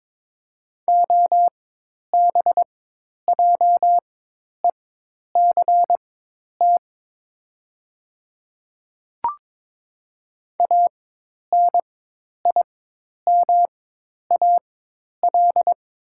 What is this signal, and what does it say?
Morse code, heard as a keyed tone, about 23 wpm, spaced as for 10 wpm, 700 Hz. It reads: OBJECT ANIMAL
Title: 4x Character Spacing - Top 500 Words - 22wpm